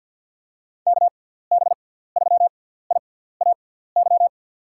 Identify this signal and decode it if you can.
Morse code, heard as a keyed tone, about 52 wpm, spaced as for 17 wpm, 700 Hz. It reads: KB3IAY
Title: USA Call Signs - 6x Character Spacing - 50wpm